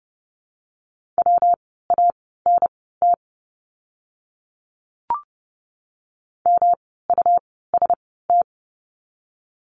Morse code, heard as a keyed tone, about 30 wpm, spaced as for 17 wpm, 700 Hz. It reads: WANT MUST
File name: course / 30:10wpm / 237 Lesson 38f - Letter q - Words 30wpm